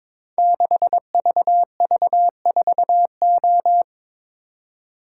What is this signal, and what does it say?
Morse code, heard as a keyed tone, about 22 wpm, 700 Hz. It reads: 6VV4O
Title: Code Groups - 5 Letter:Numbers - 22wpm